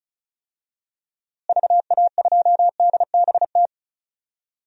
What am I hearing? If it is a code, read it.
Morse code, heard as a keyed tone, about 34 wpm, 700 Hz. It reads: VA2DBT
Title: DXCC-100 Call Signs - 6 Characters - Repeated 2x - 35wpm